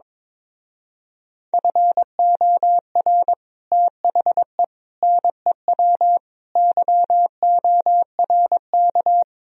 Morse code, heard as a keyed tone, about 22 wpm, 700 Hz. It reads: FOR THE NEW YORK
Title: Speed-Racing - Sets of 5 Words 22wpm